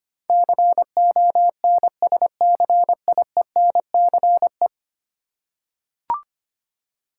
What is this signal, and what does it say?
Morse code, heard as a keyed tone, about 25 wpm, 690 Hz. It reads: CONSCIENCE
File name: 10 Letter Words - First 3 Letters Given - 25wpm